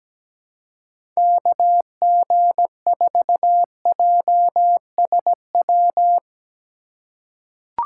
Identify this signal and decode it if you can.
Morse code, heard as a keyed tone, about 17 wpm, 700 Hz. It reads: KG4JSW
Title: USA Call Signs - 6 Characters - 2x Character Spacing - 17wpm